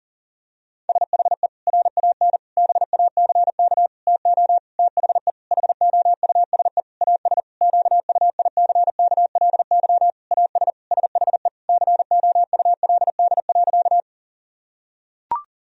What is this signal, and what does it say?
Morse code, heard as a keyed tone, about 40 wpm, 700 Hz. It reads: SHE RAN BACK TO THE HOUSE AS QUICKLY AS SHE COULD.
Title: Sentences from Top 1000 Words - Review Part 3 - 40wpm